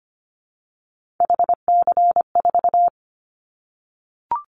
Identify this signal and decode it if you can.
Morse code, heard as a keyed tone, about 25 wpm, 700 Hz. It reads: HC4
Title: Callsign Prefixes for Top 100 DXCC - 25wpm